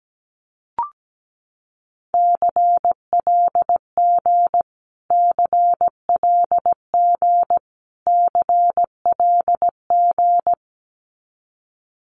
Morse code, heard as a keyed tone, about 17 wpm, 700 Hz. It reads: CLG CLG CLG
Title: Abbreviations Repeated 3x - 17wpm